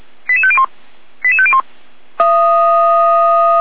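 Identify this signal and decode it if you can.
Five-tone selective call (ZVEI), heard as quick runs of five tones, tones of about 80 ms. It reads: 88501 88501
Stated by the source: ZVEI five-tone selective call received by SDR at fire brigade Orlamünde (feuerwehr-orlamuende).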